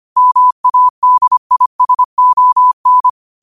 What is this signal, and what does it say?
Morse code, heard as a keyed tone, about 25 wpm, 1000 Hz. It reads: MADISON
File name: qotdfiles / 25wpm